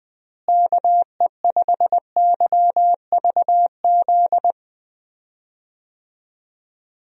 Morse code, heard as a keyed tone, about 21 wpm, 700 Hz. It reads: KE5YVZ
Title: USA Call Signs - 6 Characters - Repeated 3x - 20wpm